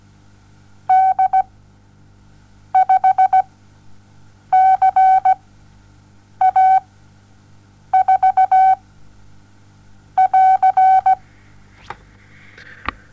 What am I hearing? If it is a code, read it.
Morse code, heard as a keyed tone, about 17 wpm, spaced as for 6 wpm, 760 Hz. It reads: D5CA4 <AR>
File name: Onlinelehrgang / morsen / DJ4UF Morsen07